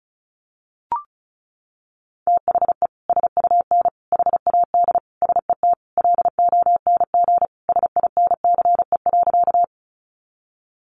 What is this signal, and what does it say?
Morse code, heard as a keyed tone, about 34 wpm, 700 Hz. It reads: THE SUN HAD SET LONG SINCE.
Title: Sentences from Top 300 Words 35wpm